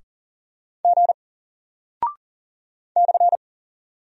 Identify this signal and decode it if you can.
Morse code, heard as a keyed tone, about 40 wpm, 700 Hz. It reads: G /
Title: Single Character (Letter:Number:Punctuation:Prosign) Alt 40wpm